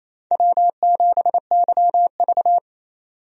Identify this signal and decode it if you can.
Morse code, heard as a keyed tone, about 28 wpm, 700 Hz. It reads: W7YV
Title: USA Call Signs - 4 Characters - 28wpm